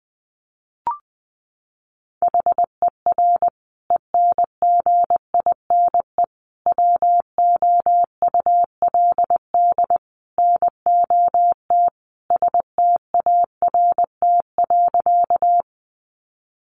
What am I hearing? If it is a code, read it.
Morse code, heard as a keyed tone, about 20 wpm, 700 Hz. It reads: HER ENGINE WOULD NOT START.
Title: Sentences from Top 1500 Words 20wpm